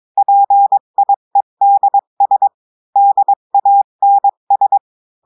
Morse code, heard as a keyed tone, about 22 wpm, 800 Hz. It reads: PIEDS DANS
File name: cours-du-mercredi / audios / cours-853-du-2-mai-2024-etxte chiffre et texte clair-a-20-wpm